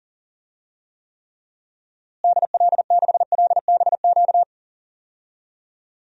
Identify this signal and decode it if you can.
Morse code, heard as a keyed tone, about 40 wpm, 700 Hz. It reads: DL6LBQ
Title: DXCC-100 Call Signs - 6 Characters - Repeated 3x - 40wpm